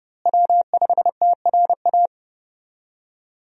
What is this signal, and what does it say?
Morse code, heard as a keyed tone, about 30 wpm, 700 Hz. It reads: W5TRA
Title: USA Call Signs - 5 Characters - Repeated 2x - 30wpm